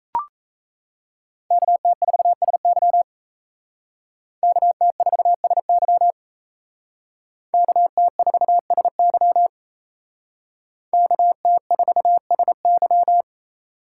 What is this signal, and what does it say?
Morse code, heard as a keyed tone, about 36 wpm, 700 Hz. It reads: KT4SY KT4SY KT4SY KT4SY